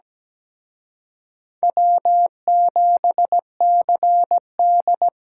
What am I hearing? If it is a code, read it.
Morse code, heard as a keyed tone, about 20 wpm, 700 Hz. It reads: W7CD
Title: Speed Racing - USA 1x2 Call Signs 17wpm